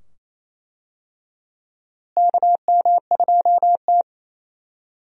Morse code, heard as a keyed tone, about 28 wpm, 700 Hz. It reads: KM2T
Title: USA Call Signs - 4 Characters - Repeated 2x - 28wpm